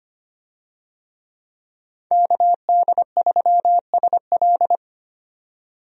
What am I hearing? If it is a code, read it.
Morse code, heard as a keyed tone, about 25 wpm, 700 Hz. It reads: KD3SL
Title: USA Call Signs - 5 Characters - 25wpm